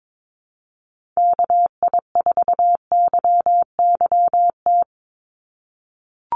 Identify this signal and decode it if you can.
Morse code, heard as a keyed tone, about 22 wpm, 700 Hz. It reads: KI4YYT
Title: USA Call Signs 22wpm